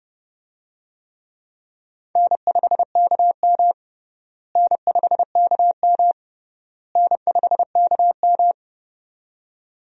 Morse code, heard as a keyed tone, about 30 wpm, 700 Hz. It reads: N5KM N5KM N5KM
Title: USA Call Signs - 4 Characters - Repeated 3x - 30wpm